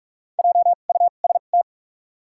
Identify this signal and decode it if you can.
Morse code, heard as a keyed tone, about 47 wpm, spaced as for 33 wpm, 700 Hz. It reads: JUST
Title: 2x Character Spacing - Top 200 Words - 45wpm